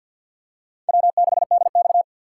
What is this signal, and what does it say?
Morse code, heard as a keyed tone, about 50 wpm, 700 Hz. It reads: W6DX
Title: USA Call Signs - 4x Character Spacing - 50wpm